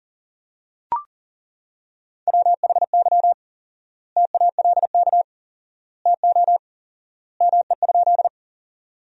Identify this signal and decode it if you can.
Morse code, heard as a keyed tone, about 40 wpm, 700 Hz. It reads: WHY TALK TO ME?
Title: 4x Word-Spacing - Sentences from Top 400 Words - 40wpm